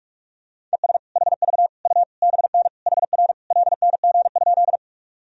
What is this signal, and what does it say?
Morse code, heard as a keyed tone, about 46 wpm, 700 Hz. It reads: ES HV U BN HR LNG?
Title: Sentences from Top 200 Words and Abbreviations 45wpm